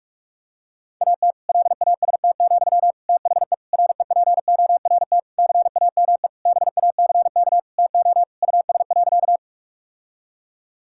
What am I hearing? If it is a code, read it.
Morse code, heard as a keyed tone, about 46 wpm, 700 Hz. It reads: AT LAST, THE REPORT CAME BACK TO US.
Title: Sentences from Top 1000 Words - Review Part 6 - 45wpm